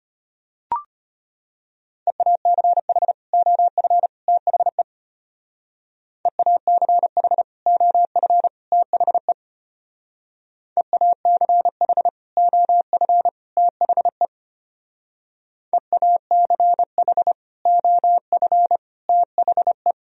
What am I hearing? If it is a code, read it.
Morse code, heard as a keyed tone, about 31 wpm, 700 Hz. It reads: EACH OF THE EACH OF THE EACH OF THE EACH OF THE
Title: Speed-Racing - Sets of 3 Words 25wpm